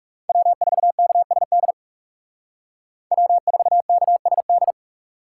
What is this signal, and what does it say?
Morse code, heard as a keyed tone, about 38 wpm, 700 Hz. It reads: W4KSD W4KSD